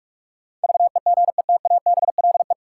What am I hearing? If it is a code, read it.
Morse code, heard as a keyed tone, about 46 wpm, 700 Hz. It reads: VEGETABLE
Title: Sentences from Top 1300 Words - New Word Drill - 45wpm